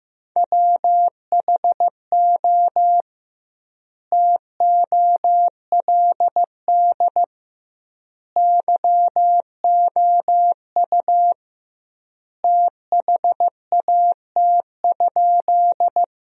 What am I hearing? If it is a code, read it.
Morse code, heard as a keyed tone, about 15 wpm, 700 Hz. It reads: WHO TOLD YOU THAT?
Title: 2x Word-Spacing - Sentences from Top 400 Words - 15wpm